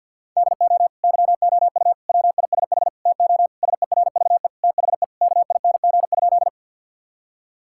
Morse code, heard as a keyed tone, about 50 wpm, 700 Hz. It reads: DO YOU WISH TO SERVE THE KING?
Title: Sentences from Top 1600 Words 50wpm